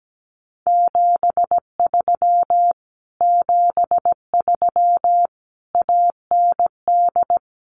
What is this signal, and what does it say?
Morse code, heard as a keyed tone, about 17 wpm, 700 Hz. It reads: 73 73 AND
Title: Common QSO Phrases Encore 17wpm